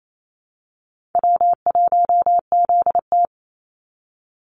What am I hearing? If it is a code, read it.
Morse code, heard as a keyed tone, about 28 wpm, 700 Hz. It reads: W1ZT
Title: Easy Call Signs - 28wpm